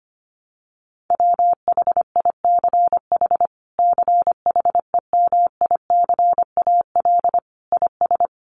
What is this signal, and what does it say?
Morse code, heard as a keyed tone, about 25 wpm, 700 Hz. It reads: WHICH CHEMICAL IS
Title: Sentences from Top 1200 Words 25wpm